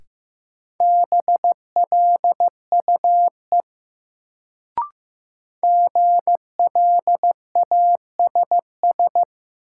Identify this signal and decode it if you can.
Morse code, heard as a keyed tone, about 15 wpm, 700 Hz. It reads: BLUE GLASS
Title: Sentences from Top 1000 Words - Review Part 3 - New Word Drill - 15wpm